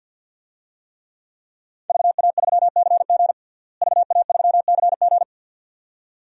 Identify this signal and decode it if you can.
Morse code, heard as a keyed tone, about 53 wpm, 700 Hz. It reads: VA3CG VA3CG
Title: DXCC-100 Call Signs - 5 Characters - Repeated 2x - 50wpm